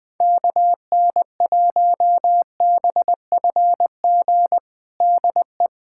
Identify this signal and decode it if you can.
Morse code, heard as a keyed tone, about 20 wpm, 700 Hz. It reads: KN1BFG DE
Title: USA Callsign DE Callsign - 6 Characters - 20wpm